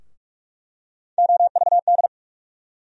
Morse code, heard as a keyed tone, about 47 wpm, 700 Hz. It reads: OVD
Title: USA Call Sign Postfixes - 3 Letters 45wpm